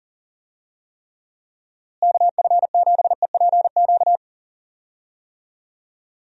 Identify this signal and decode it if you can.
Morse code, heard as a keyed tone, about 40 wpm, 700 Hz. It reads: KF7EPQ
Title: USA Call Signs - 6 Characters - Repeated 2x - 40wpm